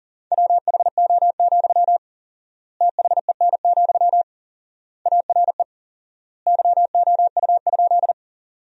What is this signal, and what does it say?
Morse code, heard as a keyed tone, about 40 wpm, 700 Hz. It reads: WHO, THEN, ARE YOU?
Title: 4x Word-Spacing - Sentences from Top 100 Words - 40wpm